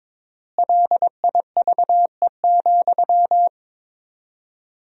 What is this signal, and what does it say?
Morse code, heard as a keyed tone, about 22 wpm, 700 Hz. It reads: LIVE,
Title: 4x Word-Spacing - Sentences from Top 200 Words - 22wpm